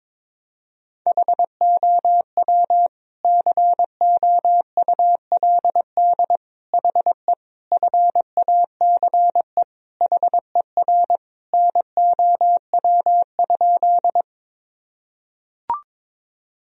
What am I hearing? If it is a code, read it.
Morse code, heard as a keyed tone, about 21 wpm, 700 Hz. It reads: HOW COULD HE FACE HER NOW?
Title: Sentences from Top 1000 Words - Review Part 2 - 22wpm